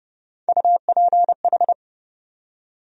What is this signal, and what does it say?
Morse code, heard as a keyed tone, about 30 wpm, 700 Hz. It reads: UPH
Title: USA Call Sign Postfixes - 3 Letters 30wpm